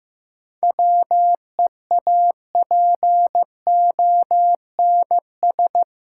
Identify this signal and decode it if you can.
Morse code, heard as a keyed tone, about 15 wpm, 700 Hz. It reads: WEAPONS
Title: Sets of 5 Words Encore - New Word Drill - 15wpm